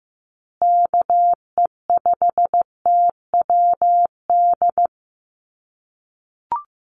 Morse code, heard as a keyed tone, about 15 wpm, 700 Hz. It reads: KE5TWD
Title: USA Call Signs Repeated 3x 15wpm